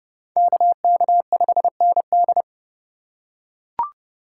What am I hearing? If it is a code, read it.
Morse code, heard as a keyed tone, about 30 wpm, 700 Hz. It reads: KK5ND
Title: USA Call Signs - 5 Characters - 3x Character Spacing - 30wpm